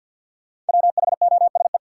Morse code, heard as a keyed tone, about 49 wpm, 700 Hz. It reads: WHOSE
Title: Sentences from Top 1200 Words - New Word Drill Alt - 50wpm